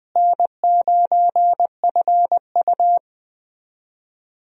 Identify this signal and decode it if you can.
Morse code, heard as a keyed tone, about 20 wpm, 700 Hz. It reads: N9FU
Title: USA Call Signs - 4 Characters - Repeated 3x - 20wpm